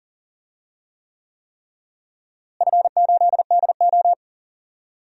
Morse code, heard as a keyed tone, about 39 wpm, 700 Hz. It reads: F8DO
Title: DXCC-100 Call Signs - 4 Characters - Repeated 3x - 40wpm